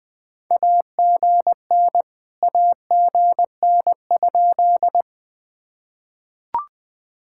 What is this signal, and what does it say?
Morse code, heard as a keyed tone, about 20 wpm, 700 Hz. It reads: AGN AGN?
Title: POTA QSO Practice - 20wpm